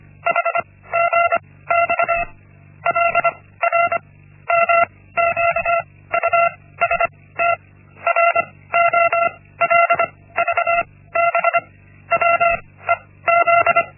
Morse code, heard as a keyed tone, about 25 wpm, 680 Hz. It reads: H G X L R M Q U S T R O L V B W E Z
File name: Linksmakalnis-1